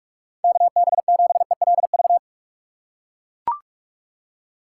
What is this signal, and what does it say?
Morse code, heard as a keyed tone, about 42 wpm, 700 Hz. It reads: KB7ELV
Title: USA Call Signs - 6 Characters - 2x Character Spacing - 45wpm